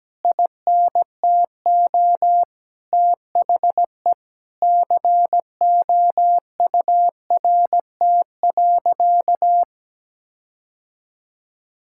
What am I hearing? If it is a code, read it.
Morse code, heard as a keyed tone, about 17 wpm, 700 Hz. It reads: INTO THE COURT.
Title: Sentences from Top 1600 Words 17wpm